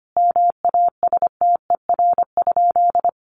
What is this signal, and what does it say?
Morse code, heard as a keyed tone, about 25 wpm, 700 Hz. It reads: MASTER?
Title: Sentences from Top 1300 Words 25wpm